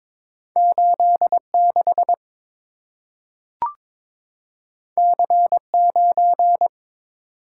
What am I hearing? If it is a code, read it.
Morse code, heard as a keyed tone, about 22 wpm, 700 Hz. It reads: 86 C9
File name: practice / 22wpm / Two Letter-Number Combinations - 22wpm